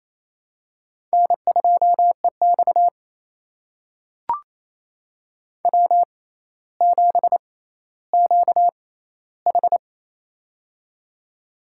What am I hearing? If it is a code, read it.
Morse code, heard as a keyed tone, about 28 wpm, spaced as for 14 wpm, 700 Hz. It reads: N2EX W7QH
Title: USA Call Signs - 4 Characters - 6x Character Spacing - 28wpm